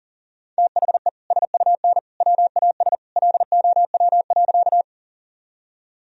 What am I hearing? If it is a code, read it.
Morse code, heard as a keyed tone, about 37 wpm, 700 Hz. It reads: THE SUN WAS LOW.
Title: Sentences from Top 1000 Words - Review Part 1 - 40wpm